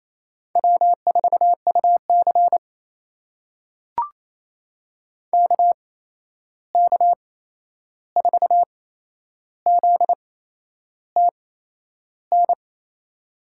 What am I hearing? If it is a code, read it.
Morse code, heard as a keyed tone, about 28 wpm, spaced as for 11 wpm, 700 Hz. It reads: W4UC KK4ZTN
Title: USA Call Signs - 8x Character Spacing - 28wpm